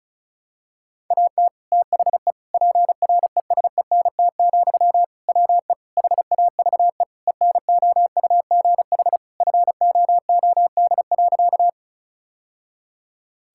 Morse code, heard as a keyed tone, about 36 wpm, 700 Hz. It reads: AT THE PRESENT, WE HAVE ENOUGH FOOD.
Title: Sentences from Top 1000 Words - Review Part 5 - 35wpm